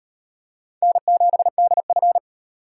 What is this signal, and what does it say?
Morse code, heard as a keyed tone, about 37 wpm, 700 Hz. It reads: N7DF